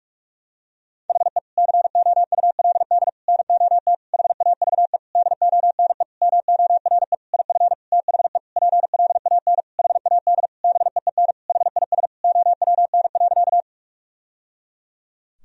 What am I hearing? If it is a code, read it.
Morse code, heard as a keyed tone, about 46 wpm, 700 Hz. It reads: HE COULD NOT HAVE DONE MORE IF THE PLAN HAD BEEN HIS OWN.
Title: Sentences from Top 1000 Words - Review Part 2 - 45wpm